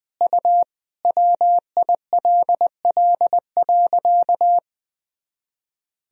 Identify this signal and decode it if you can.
Morse code, heard as a keyed tone, about 20 wpm, 700 Hz. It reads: U WILL.
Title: Sentences from Top 100 Words and Abbreviations 20wpm